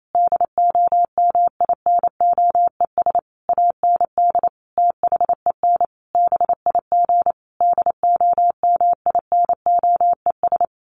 Morse code, heard as a keyed tone, about 28 wpm, 700 Hz. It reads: DOMINOES AND THEN BIG DOMINOES
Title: A Sound Of Thunder 28wpm